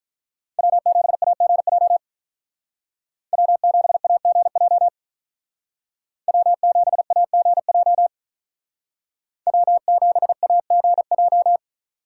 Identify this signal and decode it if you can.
Morse code, heard as a keyed tone, about 43 wpm, 700 Hz. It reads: W7AGJ W7AGJ W7AGJ W7AGJ
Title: Speed Racing - USA 1x3 Call Signs 35wpm